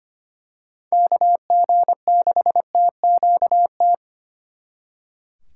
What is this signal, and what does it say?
Morse code, heard as a keyed tone, about 29 wpm, 700 Hz. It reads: KG6TQT
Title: Speed Racing - USA 2x3 Call Signs 25wpm